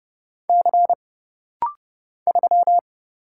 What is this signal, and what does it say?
Morse code, heard as a keyed tone, about 30 wpm, 700 Hz. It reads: C 3
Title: Single Letter-Number Alt - Rapid-Fire 30wpm